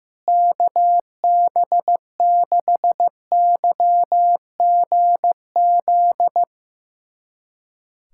Morse code, heard as a keyed tone, about 15 wpm, 700 Hz. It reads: KB6YGZ
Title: USA Callsign DE Callsign - 6 Characters - 15wpm